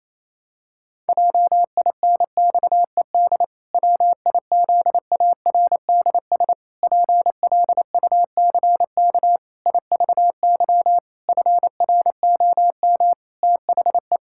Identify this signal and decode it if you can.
Morse code, heard as a keyed tone, about 28 wpm, 700 Hz. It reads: JINXED WIZARDS PLUCK IVY FROM THE
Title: Pangram Challenge v2 - 28wpm